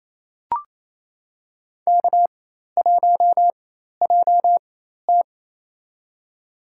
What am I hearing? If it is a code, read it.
Morse code, heard as a keyed tone, about 27 wpm, spaced as for 13 wpm, 700 Hz. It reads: K1JT
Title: USA Call Signs - 4x Character Spacing - 28wpm